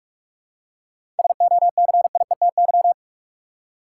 Morse code, heard as a keyed tone, about 47 wpm, 700 Hz. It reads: SOCIETY
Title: Seven Letter Words - 45wpm